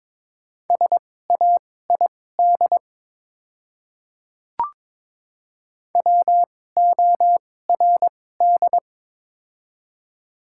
Morse code, heard as a keyed tone, about 23 wpm, spaced as for 16 wpm, 700 Hz. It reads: SAID WORD
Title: 2x Character Spacing - Top 100 Words - 22wpm